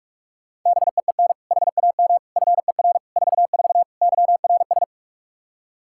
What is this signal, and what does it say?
Morse code, heard as a keyed tone, about 44 wpm, 700 Hz. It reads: BEEN HAM FER 44 YRS
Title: Practice QSO - Exchange 3 - More Info - 45wpm